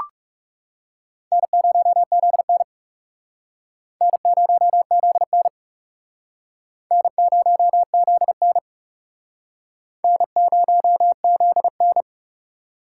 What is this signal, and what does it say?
Morse code, heard as a keyed tone, about 40 wpm, 700 Hz. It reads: N0ZN N0ZN N0ZN N0ZN